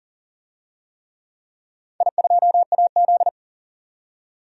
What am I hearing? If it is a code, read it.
Morse code, heard as a keyed tone, about 40 wpm, 700 Hz. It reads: I2AZ
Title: DXCC-100 Call Signs - 4 Characters - 40wpm